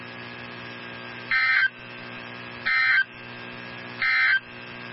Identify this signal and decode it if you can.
Emergency Alert System (SAME) tones, heard as harsh buzzing bursts. It is heard from 1.2 to 4.5 s.
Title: NOAA Weather Radio 162.550